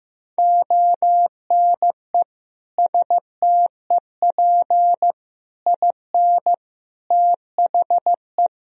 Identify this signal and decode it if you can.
Morse code, heard as a keyed tone, about 15 wpm, 700 Hz. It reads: ONE STEP IN THE
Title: Sentences from Top 1000 Words - Review Part 4 - 15wpm